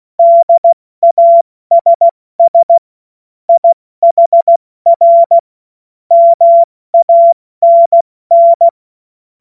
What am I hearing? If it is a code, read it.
Morse code, heard as a keyed tone, about 16 wpm, 680 Hz. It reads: DASS IHR MANN